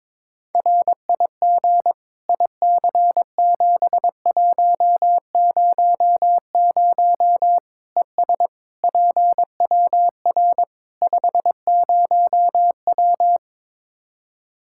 Morse code, heard as a keyed tone, about 21 wpm, 700 Hz. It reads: RIG IC7100 ES PWR 50W
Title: Practice QSO - Exchange 2 - Info - 22wpm